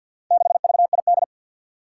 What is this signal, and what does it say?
Morse code, heard as a keyed tone, about 51 wpm, 700 Hz. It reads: 6VID
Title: Four Challenging Letter-Number Combinations 50wpm